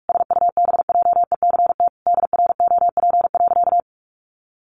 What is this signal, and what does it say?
Morse code, heard as a keyed tone, about 44 wpm, 700 Hz. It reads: SUBJECT DROP.
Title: Sentences from Top 1400 Words 45wpm